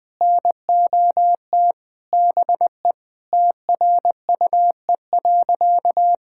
Morse code, heard as a keyed tone, about 20 wpm, 700 Hz. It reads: NOT BE TRUE.
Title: Sentences with 5-Word Phrases - Spoken Beginning - 20wpm